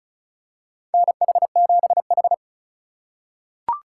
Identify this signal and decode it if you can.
Morse code, heard as a keyed tone, about 34 wpm, 700 Hz. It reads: NH7H
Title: USA Call Signs - 2x Character Spacing - 35wpm